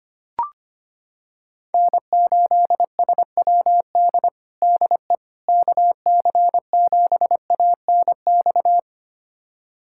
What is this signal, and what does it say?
Morse code, heard as a keyed tone, about 25 wpm, 700 Hz. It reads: N8SWD DE KC7ANX
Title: USA Callsign DE Callsign 25wpm